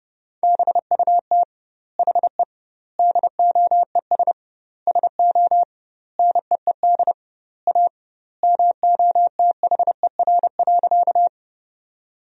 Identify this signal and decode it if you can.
Morse code, heard as a keyed tone, about 30 wpm, 700 Hz. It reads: BUT HE DOES SO NEED A MOTHER.
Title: 2x Word-Spacing - Sentences from Top 200 Words - 30wpm